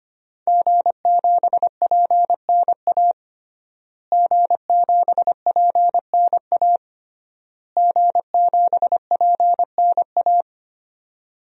Morse code, heard as a keyed tone, about 25 wpm, 700 Hz. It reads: G7PNA G7PNA G7PNA
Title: DXCC-100 Call Signs - 5 Characters - Repeated 3x - 25wpm